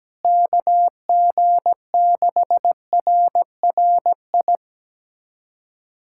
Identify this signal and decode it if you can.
Morse code, heard as a keyed tone, about 17 wpm, 700 Hz. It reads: KG6RRI
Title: USA Callsign DE Callsign - 6 Characters - 17wpm